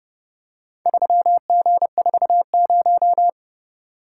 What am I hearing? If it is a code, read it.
Morse code, heard as a keyed tone, about 29 wpm, 700 Hz. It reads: 3G40